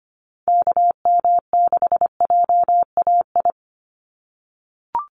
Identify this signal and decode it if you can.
Morse code, heard as a keyed tone, about 25 wpm, 700 Hz. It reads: KM6JAI
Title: USA Call Signs 25wpm